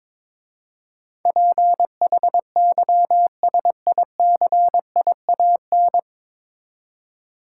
Morse code, heard as a keyed tone, about 22 wpm, 700 Hz. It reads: PHYSICIAN
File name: 9 Letter Words - First 3 Letters Given - 22wpm